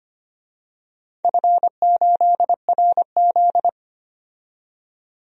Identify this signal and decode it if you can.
Morse code, heard as a keyed tone, about 25 wpm, 700 Hz. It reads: F8RZ
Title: DXCC-100 Call Signs - 4 Characters - 25wpm